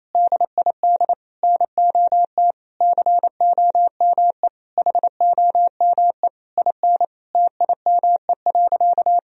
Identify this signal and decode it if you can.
Morse code, heard as a keyed tone, about 28 wpm, 700 Hz. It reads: DID NOT COME HOME IN TIME.
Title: Sentences from Top 1000 Words - Review Part 3 - 28wpm